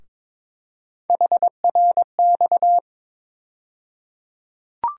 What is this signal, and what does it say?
Morse code, heard as a keyed tone, about 23 wpm, 700 Hz. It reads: HRX
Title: USA Call Sign Postfixes - 3 Letters 22wpm